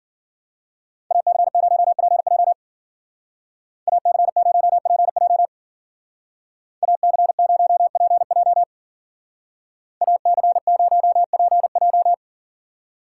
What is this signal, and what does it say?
Morse code, heard as a keyed tone, about 52 wpm, 700 Hz. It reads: AC0PJ AC0PJ AC0PJ AC0PJ